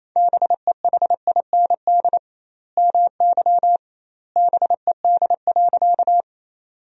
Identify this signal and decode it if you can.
Morse code, heard as a keyed tone, about 28 wpm, 700 Hz. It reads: BEHIND MY BED.
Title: 2x Word-Spacing - Sentences from Top 500 Words - 28wpm